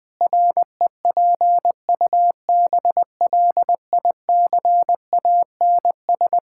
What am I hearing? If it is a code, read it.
Morse code, heard as a keyed tone, about 21 wpm, 700 Hz. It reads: REPUBLICANS
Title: Pangram Challenge v2 - 20wpm